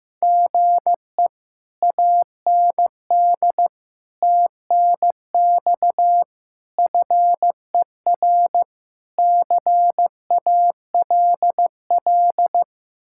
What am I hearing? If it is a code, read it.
Morse code, heard as a keyed tone, about 15 wpm, 700 Hz. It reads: GE AND TNX FER CALL
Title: Speed-Racing Common QSO Phrases Encore - 15wpm